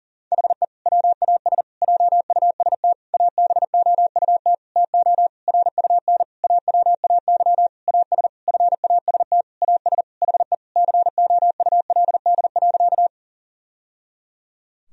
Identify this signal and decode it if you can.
Morse code, heard as a keyed tone, about 40 wpm, 700 Hz. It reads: HE WAS JUST ABOUT TO RUN AWAY AS FAST AS HE COULD.
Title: Sentences from Top 1000 Words - Review Part 2 - 40wpm